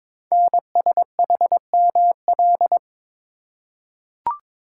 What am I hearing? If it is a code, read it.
Morse code, heard as a keyed tone, about 22 wpm, 700 Hz. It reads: NSHML
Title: Code Groups - 5 Letters - 22wpm